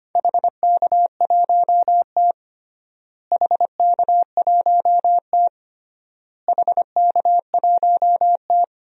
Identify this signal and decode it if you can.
Morse code, heard as a keyed tone, about 25 wpm, 700 Hz. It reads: HK1T HK1T HK1T
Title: DXCC-100 Call Signs - 4 Characters - Repeated 3x - 25wpm